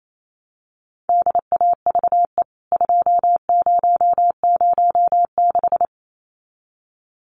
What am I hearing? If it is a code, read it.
Morse code, heard as a keyed tone, about 28 wpm, 700 Hz. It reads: DAVE 2006
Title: CWT Exchange 28wpm